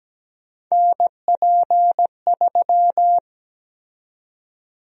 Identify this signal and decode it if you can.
Morse code, heard as a keyed tone, about 17 wpm, 700 Hz. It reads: NP3
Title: USA Call Sign Prefixes - Repeated 2x - 17wpm